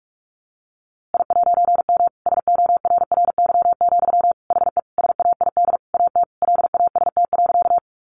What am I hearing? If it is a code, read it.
Morse code, heard as a keyed tone, about 44 wpm, 700 Hz. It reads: I'M SORRY, HE SAID AT LAST.
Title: A Sound Of Thunder 45wpm